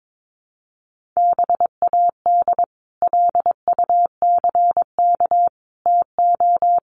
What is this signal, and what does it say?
Morse code, heard as a keyed tone, about 22 wpm, 700 Hz. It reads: BAD LUCK TO
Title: Sentences from Top 1500 Words 22wpm